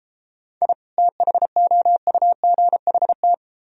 Morse code, heard as a keyed tone, about 32 wpm, 700 Hz. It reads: I THOUGHT